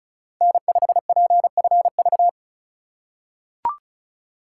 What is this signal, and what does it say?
Morse code, heard as a keyed tone, about 34 wpm, 700 Hz. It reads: N5PFV
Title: USA Call Signs - 2x Character Spacing - 35wpm